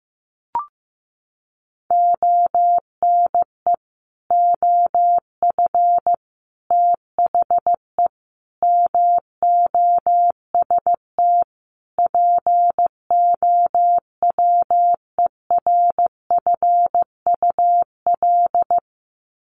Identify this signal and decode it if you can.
Morse code, heard as a keyed tone, about 15 wpm, 700 Hz. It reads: ONE OF THE MOST POWERFUL
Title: Sets of 5 Words Encore 15wpm